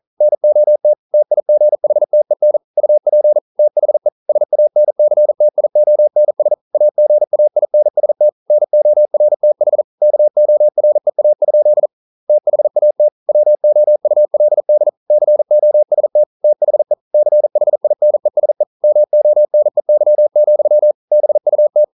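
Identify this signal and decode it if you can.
Morse code, heard as a keyed tone, about 40 wpm, 600 Hz. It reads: NOT TIGHTEN UP THE SANCTIONS AGAINST NORTH KOREA? THAT WOULD COST THE CHINESE MONEY, BUT
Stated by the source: Morse code practice at 41 wpm, from CW med Gnister.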